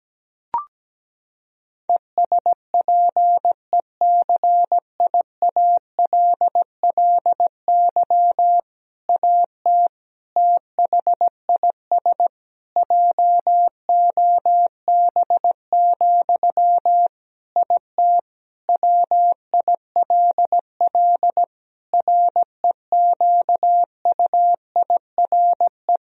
Morse code, heard as a keyed tone, about 17 wpm, 700 Hz. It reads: ESPECIALLY AT THIS JOB, IT WILL REQUIRE
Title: Sentences from Top 1000 Words - Review Part 4 - 17wpm